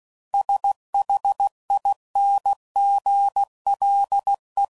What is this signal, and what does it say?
Morse code, heard as a keyed tone, about 16 wpm, 800 Hz. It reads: SHINGLE